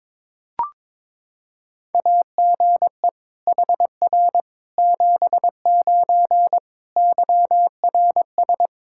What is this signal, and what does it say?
Morse code, heard as a keyed tone, about 22 wpm, 700 Hz. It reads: AGE HR 79 YRS
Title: Practice QSO - Exchange 3 - More Info - 22wpm